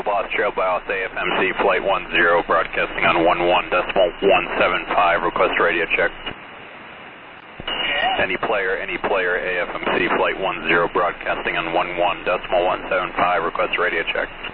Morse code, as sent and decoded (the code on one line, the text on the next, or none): none